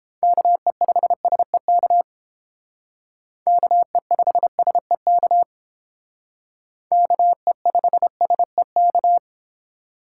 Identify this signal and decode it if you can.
Morse code, heard as a keyed tone, about 30 wpm, 700 Hz. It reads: KE5SEK KE5SEK KE5SEK